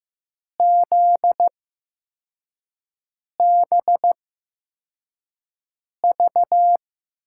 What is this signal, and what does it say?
Morse code, heard as a keyed tone, about 15 wpm, spaced as for 4 wpm, 700 Hz. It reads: ZBV
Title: USA Call Signs - 6 Characters - 8x Character Spacing - 15wpm